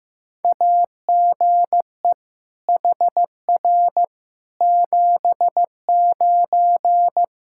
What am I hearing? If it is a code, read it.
Morse code, heard as a keyed tone, about 15 wpm, 700 Hz. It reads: AGE HR 79
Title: Practice QSO - Exchange 3 - More Info - 15wpm